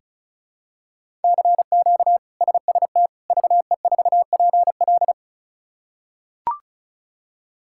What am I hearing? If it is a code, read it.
Morse code, heard as a keyed tone, about 36 wpm, 700 Hz. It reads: CQ SST VE4PL
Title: K1USN SST QSO Practice - 35wpm